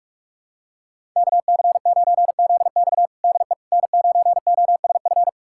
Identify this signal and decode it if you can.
Morse code, heard as a keyed tone, about 46 wpm, 700 Hz. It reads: KC9ZX DE N9OSF
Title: USA Callsign DE Callsign - 5 Characters - 45wpm